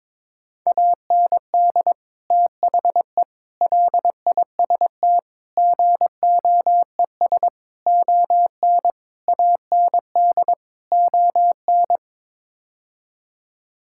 Morse code, heard as a keyed tone, about 22 wpm, 700 Hz. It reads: AND THE LIST GOES ON AND ON
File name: practice / 22wpm / Sets of 7 Words - 22wpm